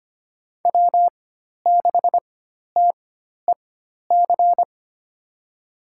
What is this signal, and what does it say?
Morse code, heard as a keyed tone, about 25 wpm, spaced as for 12 wpm, 700 Hz. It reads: W6TEC
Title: USA Call Signs - 5 Characters - 4x Character Spacing - 25wpm